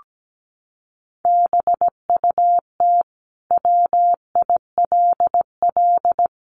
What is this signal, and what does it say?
Morse code, heard as a keyed tone, about 17 wpm, 700 Hz. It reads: BUT WILL
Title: Sentences from Top 100 Words 17wpm